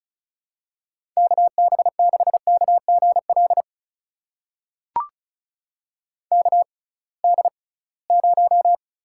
Morse code, heard as a keyed tone, about 36 wpm, spaced as for 23 wpm, 700 Hz. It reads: KB6KGL KD0
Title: USA Call Signs - 6 Characters - 6x Character Spacing - 35wpm